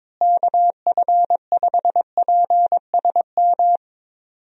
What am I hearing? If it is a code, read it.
Morse code, heard as a keyed tone, about 21 wpm, 700 Hz. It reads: KF5PSM
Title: USA Call Signs - 6 Characters - Repeated 2x - 22wpm